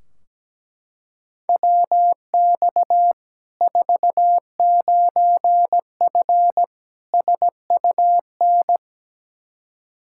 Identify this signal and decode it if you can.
Morse code, heard as a keyed tone, about 17 wpm, 700 Hz. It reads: WX 49F SUN
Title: Practice QSO - Exchange 2 - Info - 17wpm